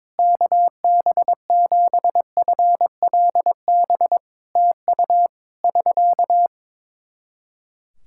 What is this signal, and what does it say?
Morse code, heard as a keyed tone, about 21 wpm, 700 Hz. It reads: KB7FLB TU <SK>
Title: Practice QSO - Exchange 4 - Ending - 22wpm